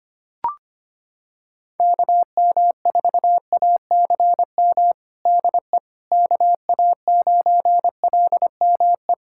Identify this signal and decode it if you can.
Morse code, heard as a keyed tone, about 25 wpm, 700 Hz. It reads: KM4ACM DE KA9LME
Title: USA Callsign DE Callsign 25wpm